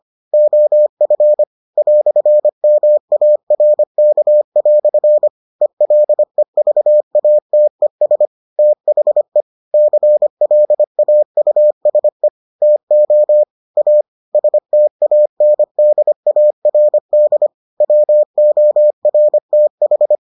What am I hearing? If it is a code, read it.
Morse code, heard as a keyed tone, about 25 wpm, 600 Hz. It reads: OF "MARK" ELEVATES THE CLAUSE TO A STANDARD WORTH